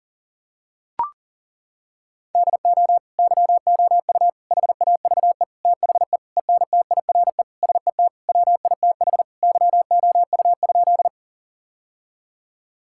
Morse code, heard as a keyed tone, about 40 wpm, 700 Hz. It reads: DO YOU HAVE THE ENTIRE SET WITH YOU?
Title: Sentences from Top 1000 Words - Review Part 5 - 40wpm